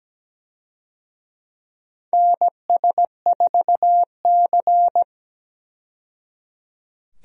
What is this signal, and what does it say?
Morse code, heard as a keyed tone, about 17 wpm, 700 Hz. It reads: NS4C
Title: USA Call Signs - 4 Characters - 17wpm